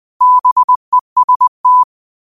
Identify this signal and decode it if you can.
Morse code, heard as a keyed tone, about 20 wpm, 1000 Hz. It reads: BEST
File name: qotdfiles / 20wpm